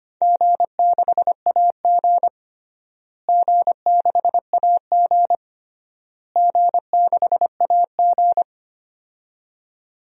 Morse code, heard as a keyed tone, about 25 wpm, 700 Hz. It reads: G6AG G6AG G6AG
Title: DXCC-100 Call Signs - 4 Characters - Repeated 3x - 25wpm